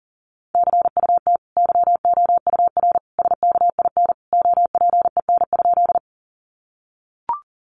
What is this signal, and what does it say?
Morse code, heard as a keyed tone, about 40 wpm, 700 Hz. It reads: CUT YOUR SKIN OPEN?
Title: Sentences from Top 1300 Words 40wpm